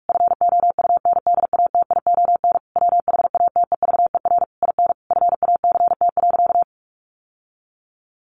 Morse code, heard as a keyed tone, about 44 wpm, 700 Hz. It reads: FOUNDATION WHATEVER IN FACT.
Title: Sentences from Top 1200 Words 45wpm